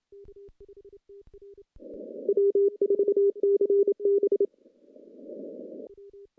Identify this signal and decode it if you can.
Morse code, heard as a keyed tone, about 27 wpm, 410 Hz. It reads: W4CB
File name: CW-SS-k5tr